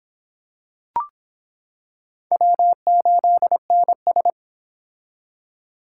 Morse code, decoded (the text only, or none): W8NS